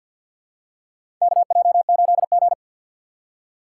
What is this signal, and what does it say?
Morse code, heard as a keyed tone, about 45 wpm, 700 Hz. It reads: KJ8G
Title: USA Call Signs - 4 Characters - 3x Character Spacing - 50wpm